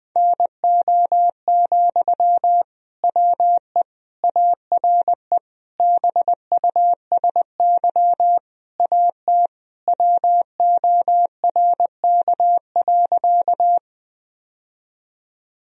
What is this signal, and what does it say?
Morse code, heard as a keyed tone, about 20 wpm, 700 Hz. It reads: NO, WE ARE BUSY AT WORK.
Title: Sentences from Top 1000 Words - Review Part 3 - 20wpm